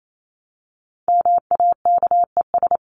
Morse code, heard as a keyed tone, about 28 wpm, 700 Hz. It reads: MAKES
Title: ES:S Suffix - Top 100 Words with Base Words 28wpm